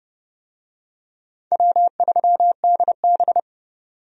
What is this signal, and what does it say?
Morse code, heard as a keyed tone, about 29 wpm, 700 Hz. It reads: W3DB
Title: USA Call Signs - 4 Characters - 30wpm